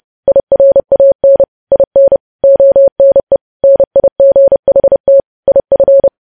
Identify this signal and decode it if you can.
Morse code, heard as a keyed tone, about 30 wpm, 550 Hz. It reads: IRAN IN ONE NIGHT IF